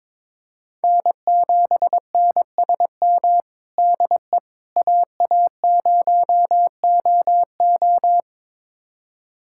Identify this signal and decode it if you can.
Morse code, heard as a keyed tone, about 21 wpm, 700 Hz. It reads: N7NSM DE AA0OO
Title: USA Callsign DE Callsign - 5 Characters - 22wpm